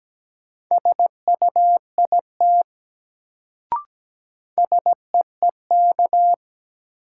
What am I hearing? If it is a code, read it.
Morse code, heard as a keyed tone, about 17 wpm, 700 Hz. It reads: SUIT SEEK
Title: Sentences from Top 1600 Words - New Word Drill - 17wpm